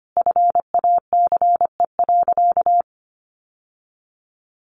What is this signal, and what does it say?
Morse code, heard as a keyed tone, about 25 wpm, 700 Hz. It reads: FACE.
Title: Sentences from Top 1100 Words 25wpm